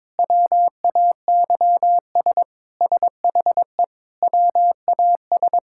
Morse code, heard as a keyed tone, about 22 wpm, 700 Hz. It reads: WAYS SHE WAS
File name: practice / 22wpm / Sentences with 4-Word Phrases - 22wpm